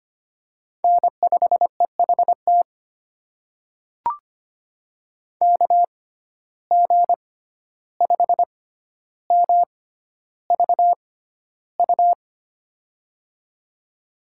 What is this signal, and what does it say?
Morse code, heard as a keyed tone, about 25 wpm, spaced as for 12 wpm, 700 Hz. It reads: N5EHT KG5MVU
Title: USA Call Signs - 6x Character Spacing - 25wpm